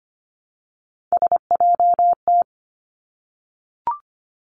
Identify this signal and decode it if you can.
Morse code, heard as a keyed tone, about 25 wpm, 700 Hz. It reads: SJT
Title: Three Challenging Letter-Number Combinations 25wpm